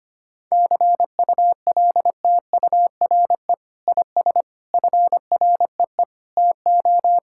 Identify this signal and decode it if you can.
Morse code, heard as a keyed tone, about 25 wpm, 700 Hz. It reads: CULTURE IS FREE TO
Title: Sentences from Top 1000 Words - Review Part 6 - 25wpm